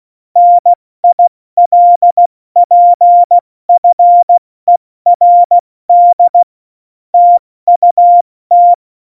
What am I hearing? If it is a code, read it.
Morse code, heard as a keyed tone, about 16 wpm, spaced as for 14 wpm, 700 Hz. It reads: NILPFERD TUT
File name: Morsekurs / Witze / Witz 08. Dezember 2020, 70 F80